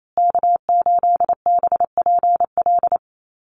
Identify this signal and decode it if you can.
Morse code, heard as a keyed tone, about 28 wpm, 700 Hz. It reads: K8BPL
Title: USA Call Signs Repeated 2x 28wpm